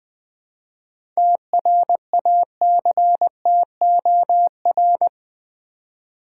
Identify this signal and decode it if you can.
Morse code, heard as a keyed tone, about 20 wpm, 700 Hz. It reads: TRACTOR
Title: Easy Words with Hints - 20wpm